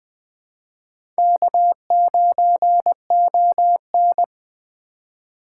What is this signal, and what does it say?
Morse code, heard as a keyed tone, about 20 wpm, 700 Hz. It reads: K9ON
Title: Speed Racing - USA 1x2 Call Signs 17wpm